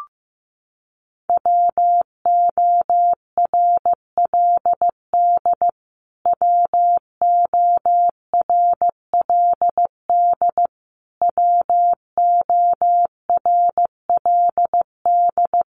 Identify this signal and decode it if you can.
Morse code, heard as a keyed tone, about 15 wpm, 700 Hz. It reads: WORLD WORLD WORLD
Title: Top 500 Words Repeated 15wpm